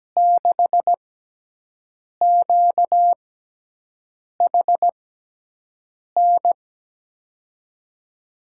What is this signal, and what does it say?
Morse code, heard as a keyed tone, about 17 wpm, spaced as for 6 wpm, 700 Hz. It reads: 6QHN
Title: USA Call Signs - 6 Characters - 6x Character Spacing - 17wpm